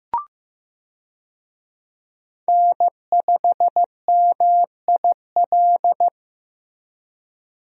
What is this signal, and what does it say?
Morse code, heard as a keyed tone, about 15 wpm, 700 Hz. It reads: N5MIL
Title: USA Call Signs - 5 Characters - Repeated 3x - 15wpm